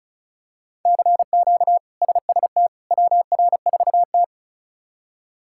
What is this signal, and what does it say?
Morse code, heard as a keyed tone, about 34 wpm, 700 Hz. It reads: CQ SST WR4T
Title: K1USN SST QSO Practice - 35wpm